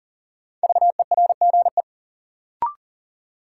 Morse code, heard as a keyed tone, about 40 wpm, 700 Hz. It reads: VERGE
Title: Sets of 4 Words - New Word Drill - 40wpm